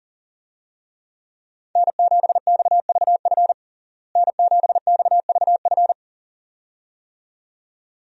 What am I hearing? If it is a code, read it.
Morse code, heard as a keyed tone, about 40 wpm, 700 Hz. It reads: N7XVF N7XVF
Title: USA Call Signs - 5 Characters - Repeated 2x - 40wpm